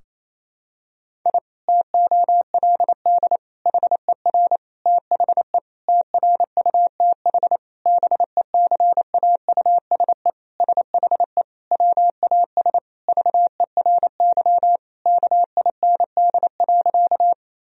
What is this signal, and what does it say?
Morse code, heard as a keyed tone, about 28 wpm, 700 Hz. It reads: I TOLD HER THE TRUTH BECAUSE SHE WAS VERY KIND.
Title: Sentences from Top 1000 Words - Review Part 3 - 28wpm